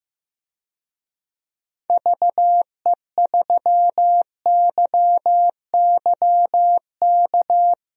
Morse code, heard as a keyed tone, about 15 wpm, 700 Hz. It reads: VE3YYK
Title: DXCC-100 Call Signs - 6 Characters - Repeated 3x - 15wpm